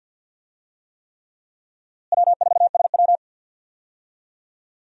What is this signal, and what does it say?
Morse code, heard as a keyed tone, about 49 wpm, 700 Hz. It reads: W4SW
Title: USA Call Signs - 4 Characters - Repeated 3x - 50wpm